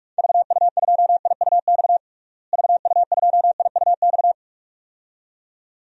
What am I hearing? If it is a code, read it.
Morse code, heard as a keyed tone, about 47 wpm, 700 Hz. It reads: VU2IUX VU2IUX
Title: DXCC-100 Call Signs - 6 Characters - Repeated 2x - 45wpm